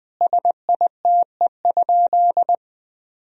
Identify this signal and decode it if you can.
Morse code, heard as a keyed tone, about 20 wpm, 700 Hz. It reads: SITE?
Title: Sentences from Top 1000 Words - Review Part 4 - 20wpm